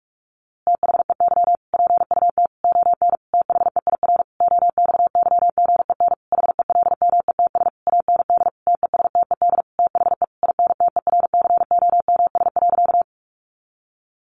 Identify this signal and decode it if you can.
Morse code, heard as a keyed tone, about 45 wpm, 700 Hz. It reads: THEY PUT ON THEIR OXYGEN HELMETS AND TESTED THE INTERCOMS.
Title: A Sound Of Thunder 45wpm